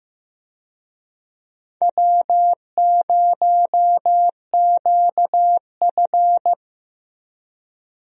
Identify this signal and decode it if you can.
Morse code, heard as a keyed tone, about 15 wpm, 700 Hz. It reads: W0QF
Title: USA Call Signs - 4 Characters - Repeated 2x - 15wpm